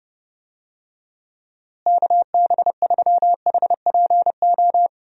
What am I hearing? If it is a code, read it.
Morse code, heard as a keyed tone, about 30 wpm, 700 Hz. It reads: KB3HPO
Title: USA Call Signs - 6 Characters - Repeated 3x - 30wpm